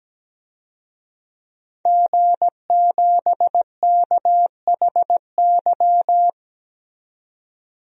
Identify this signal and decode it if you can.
Morse code, heard as a keyed tone, about 17 wpm, 700 Hz. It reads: G7KHY
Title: DXCC-100 Call Signs - 5 Characters - 17wpm